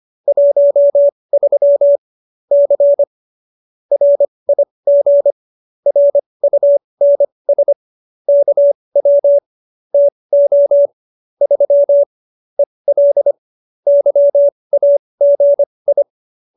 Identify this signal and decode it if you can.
Morse code, heard as a keyed tone, about 25 wpm, spaced as for 20 wpm, 570 Hz. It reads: 13 C RIG RUNS KW TO 3 EL YAGI